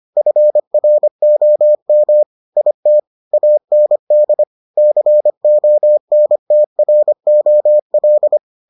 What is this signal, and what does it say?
Morse code, heard as a keyed tone, about 25 wpm, 600 Hz. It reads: FROM IT AND CONTROL